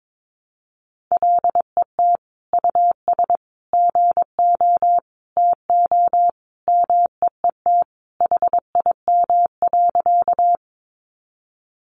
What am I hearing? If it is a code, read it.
Morse code, heard as a keyed tone, about 22 wpm, 700 Hz. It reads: LET US GO TO MEET HIM.
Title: Sentences from Top 1100 Words 22wpm